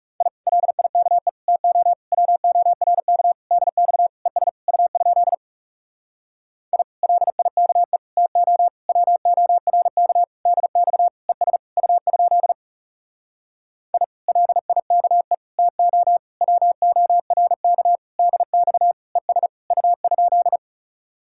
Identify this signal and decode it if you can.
Morse code, heard as a keyed tone, about 40 wpm, 700 Hz. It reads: I LIKE TO WORK DX ES U? I LIKE TO WORK DX ES U? I LIKE TO WORK DX ES U?